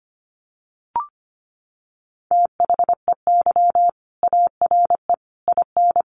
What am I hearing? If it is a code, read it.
Morse code, heard as a keyed tone, about 25 wpm, 700 Hz. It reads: THEY ARE IN
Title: Sentences from Top 1400 Words 25wpm